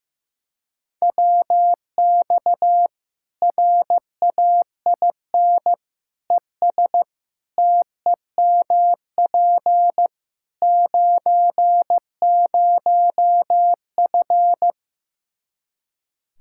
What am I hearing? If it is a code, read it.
Morse code, heard as a keyed tone, about 15 wpm, 700 Hz. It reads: WX RAIN ES TEMP 90F
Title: Practice QSO - Exchange 2 - Info - 15wpm